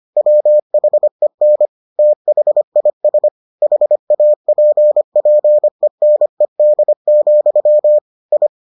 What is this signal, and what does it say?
Morse code, heard as a keyed tone, about 25 wpm, 600 Hz. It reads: WHEN THIS HAPPENED, I